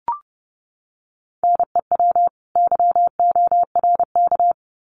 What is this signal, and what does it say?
Morse code, heard as a keyed tone, about 30 wpm, 700 Hz. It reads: NEW YORK
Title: Sets of 2 Words 30wpm